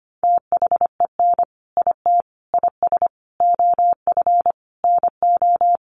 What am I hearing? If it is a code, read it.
Morse code, heard as a keyed tone, about 25 wpm, 700 Hz. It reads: THEN IT IS OF NO
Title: Sentences from Top 1400 Words 25wpm